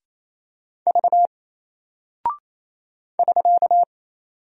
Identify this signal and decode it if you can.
Morse code, heard as a keyed tone, about 28 wpm, 700 Hz. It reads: V <SK>
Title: Single Character (Letter:Number:Punctuation:Prosign) Alt 28wpm